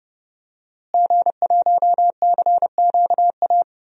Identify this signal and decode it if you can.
Morse code, heard as a keyed tone, about 30 wpm, 700 Hz. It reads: G1CQA
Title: DXCC-100 Call Signs - 5 Characters - Repeated 3x - 30wpm